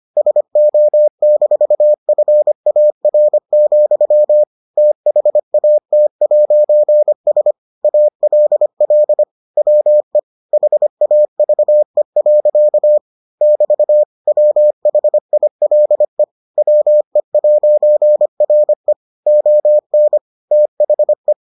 Morse code, heard as a keyed tone, about 25 wpm, 600 Hz. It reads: SO-FAR, THAT'S ALL WE HAVE. <BT> WHILE WE'RE ON THE